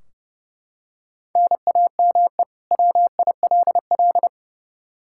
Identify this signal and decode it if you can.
Morse code, heard as a keyed tone, about 30 wpm, 700 Hz. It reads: NAME WILL
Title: K1USN SST QSO Practice 30wpm